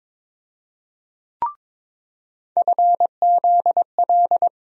Morse code, heard as a keyed tone, about 22 wpm, 700 Hz. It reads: FZL
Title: USA Call Sign Postfixes - 3 Letters 22wpm